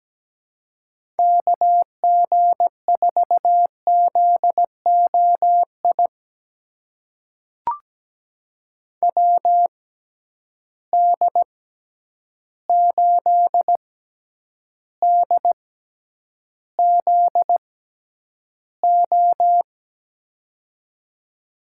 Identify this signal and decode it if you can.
Morse code, heard as a keyed tone, about 17 wpm, spaced as for 9 wpm, 700 Hz. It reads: KG4ZOI WD8DZO
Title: USA Call Signs - 6 Characters - 6x Character Spacing - 17wpm